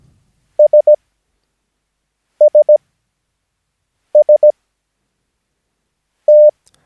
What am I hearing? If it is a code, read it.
Morse code, heard as a keyed tone, about 17 wpm, spaced as for 5 wpm, 600 Hz. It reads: SSST
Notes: Övningsmaterial med alfabetet A till Ö samt siffror 0 till 9 enligt International/Continental Morse Code.